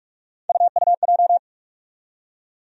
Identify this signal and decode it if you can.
Morse code, heard as a keyed tone, about 48 wpm, 700 Hz. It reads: UUJ
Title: USA Call Sign Postfixes - 3 Letters 45wpm